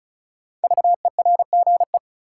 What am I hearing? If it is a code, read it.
Morse code, heard as a keyed tone, about 36 wpm, 700 Hz. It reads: VERGE
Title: Sets of 4 Words - New Word Drill - 35wpm